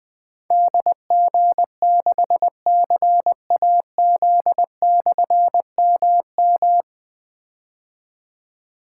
Morse code, heard as a keyed tone, about 20 wpm, 700 Hz. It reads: DG6CAZ/MM
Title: Difficult Call Signs 20wpm